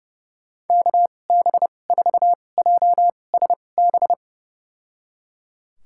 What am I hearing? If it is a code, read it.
Morse code, heard as a keyed tone, about 30 wpm, spaced as for 22 wpm, 700 Hz. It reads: KB4JSB
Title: USA Call Signs - 6 Characters - 2x Character Spacing - 30wpm